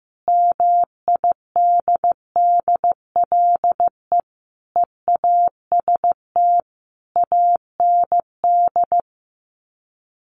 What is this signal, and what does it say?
Morse code, heard as a keyed tone, about 15 wpm, 700 Hz. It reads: MIDDLE EAST AND
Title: Sets of 5 Words Encore 15wpm